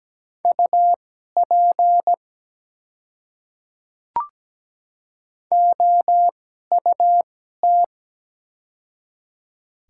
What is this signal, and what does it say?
Morse code, heard as a keyed tone, about 17 wpm, spaced as for 12 wpm, 700 Hz. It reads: UP OUT
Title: 2x Character Spacing - Top 300 Words - 17wpm